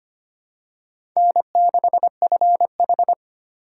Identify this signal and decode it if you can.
Morse code, heard as a keyed tone, about 25 wpm, 700 Hz. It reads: N6FH
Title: USA Call Signs - 2x Character Spacing - 25wpm